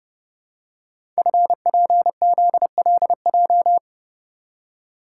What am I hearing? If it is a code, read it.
Morse code, heard as a keyed tone, about 30 wpm, 700 Hz. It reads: FPZLJ
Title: Code Groups - 5 Letter:Numbers - 30wpm